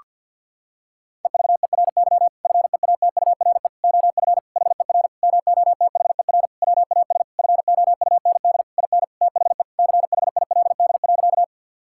Morse code, heard as a keyed tone, about 50 wpm, 700 Hz. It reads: EVERY FEATURE OF HER MOTHER WAS FOUND IN THE CHILD.
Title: Sentences from Top 1000 Words - Review Part 5 - 50wpm